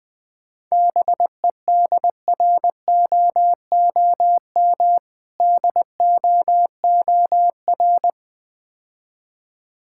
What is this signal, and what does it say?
Morse code, heard as a keyed tone, about 20 wpm, 700 Hz. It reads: BEDROOM DOOR
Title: Difficult Words - Speed-Racing with Hints - 20wpm